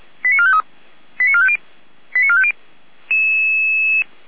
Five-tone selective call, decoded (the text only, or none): 89453 89450 89450